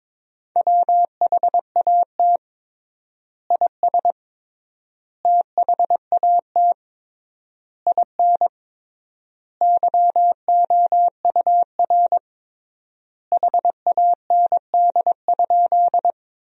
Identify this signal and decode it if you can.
Morse code, heard as a keyed tone, about 22 wpm, 700 Hz. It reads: WHAT IS THAT IN YOUR HAND?
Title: 3x Word-Spacing - Sentences from Top 200 Words - 22wpm